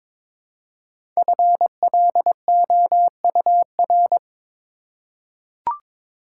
Morse code, heard as a keyed tone, about 23 wpm, 700 Hz. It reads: FLOUR
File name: 5 Letter Words - First 3 Letters Given - 22wpm